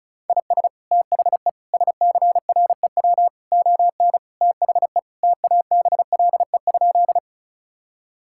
Morse code, heard as a keyed tone, about 34 wpm, 700 Hz. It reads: IS THE SCREW ON THE TABLE?
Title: Sentences from Top 1600 Words 35wpm